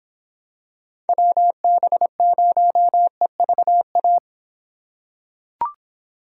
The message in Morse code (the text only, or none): WB0EVA